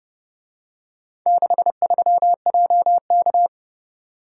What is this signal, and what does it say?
Morse code, heard as a keyed tone, about 29 wpm, 700 Hz. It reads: 63JK